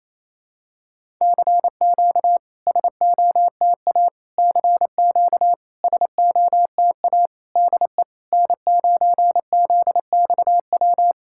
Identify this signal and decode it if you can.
Morse code, heard as a keyed tone, about 28 wpm, 700 Hz. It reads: CQ SOTA CQ SOTA DE N9ZXW
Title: SOTA QSO Practice 28wpm